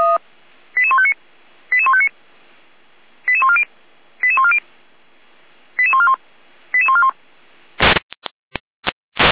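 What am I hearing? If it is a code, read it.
Five-tone selective call (ZVEI), heard as quick runs of five tones, tones of about 70 ms. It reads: 88159 88159 88140 88140 88141 88141
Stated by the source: ZVEI five-tone selective call 88501, received by SDR, Feuerwehr Orlamünde.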